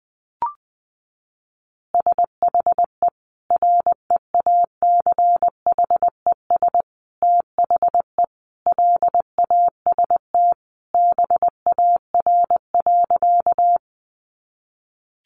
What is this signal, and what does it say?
Morse code, heard as a keyed tone, about 20 wpm, 700 Hz. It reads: SHE REACHES THE LAST BAR.
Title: Easy Phrases 20wpm